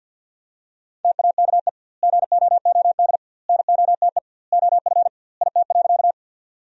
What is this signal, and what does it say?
Morse code, heard as a keyed tone, about 52 wpm, 700 Hz. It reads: TAKE GOOD NOTE OF IT.
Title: 2x Word-Spacing - Sentences from Top 500 Words - 50wpm